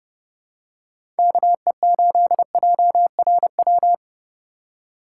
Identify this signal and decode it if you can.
Morse code, heard as a keyed tone, about 29 wpm, 700 Hz. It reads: KE8JRW
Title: USA Call Signs - 2x Character Spacing - 30wpm